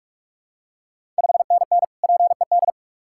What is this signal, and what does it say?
Morse code, heard as a keyed tone, about 46 wpm, 700 Hz. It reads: 5NN PED
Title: Ontario QSO Party Exchange 45wpm